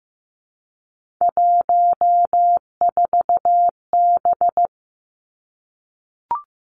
Morse code, heard as a keyed tone, about 15 wpm, 700 Hz. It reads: 14B
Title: Three Challenging Letter-Number Combinations 15wpm